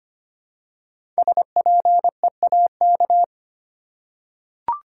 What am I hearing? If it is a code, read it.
Morse code, heard as a keyed tone, about 25 wpm, 700 Hz. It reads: SPEAK
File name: 5 Letter Words - First 3 Letters Given - 25wpm